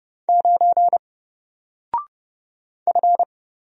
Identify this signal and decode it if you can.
Morse code, heard as a keyed tone, about 30 wpm, 700 Hz. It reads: 9 F
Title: Single Character (Letter:Number:Punctuation:Prosign) Alt 30wpm